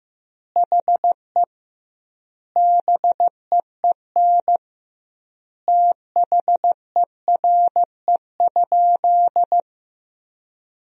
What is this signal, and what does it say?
Morse code, heard as a keyed tone, about 15 wpm, 700 Hz. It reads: HE BEEN THERE?
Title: 2x Word-Spacing - Sentences from Top 100 Words - 15wpm